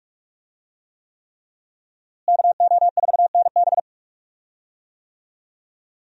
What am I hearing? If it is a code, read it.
Morse code, heard as a keyed tone, about 43 wpm, 700 Hz. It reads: KO4NB
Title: USA Call Signs - 5 Characters - Repeated 3x - 45wpm